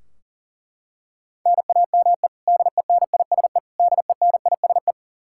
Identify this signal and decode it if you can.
Morse code, heard as a keyed tone, about 40 wpm, 700 Hz. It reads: NAME DENISE DENISE
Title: Practice QSO - Exchange 1 - Beginning - 40wpm